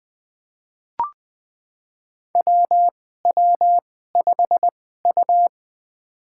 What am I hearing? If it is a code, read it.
Morse code, heard as a keyed tone, about 21 wpm, spaced as for 15 wpm, 700 Hz. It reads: WW5U
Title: USA Call Signs - 2x Character Spacing - 20wpm